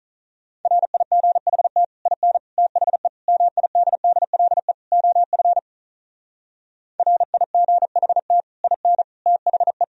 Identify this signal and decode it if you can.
Morse code, heard as a keyed tone, about 42 wpm, 700 Hz. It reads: RIGHT IN THE MIDDLE OF RIGHT IN THE
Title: Speed-Racing - Sets of 5 Words 35wpm